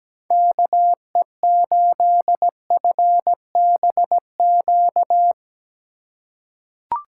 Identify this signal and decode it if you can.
Morse code, heard as a keyed tone, about 17 wpm, 700 Hz. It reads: KE8FBQ
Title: USA Call Signs - 6 Characters - 4x Character Spacing - 17wpm